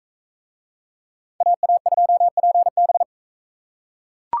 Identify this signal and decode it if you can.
Morse code, heard as a keyed tone, about 44 wpm, 700 Hz. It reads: AA2PB